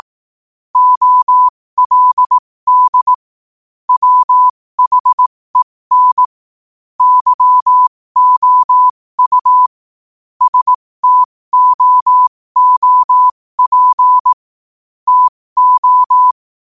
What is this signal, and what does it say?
Morse code, heard as a keyed tone, about 18 wpm, spaced as for 15 wpm, 1000 Hz. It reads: OLD WHEN YOU STOOP TO
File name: qotdfiles / 15wpm